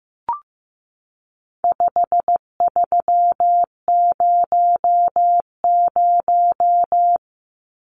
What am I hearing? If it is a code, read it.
Morse code, heard as a keyed tone, about 15 wpm, 700 Hz. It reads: 5300
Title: Numbers - 4-Digit 15wpm